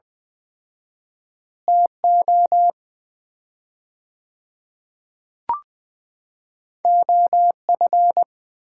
Morse code, heard as a keyed tone, about 20 wpm, 700 Hz. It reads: TO OF
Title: 4x Word-Spacing - Sets of 2 Words - 20wpm